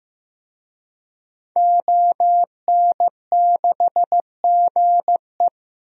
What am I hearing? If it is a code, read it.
Morse code, heard as a keyed tone, about 15 wpm, 700 Hz. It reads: ON6GE
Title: DXCC-100 Call Signs - 5 Characters - 15wpm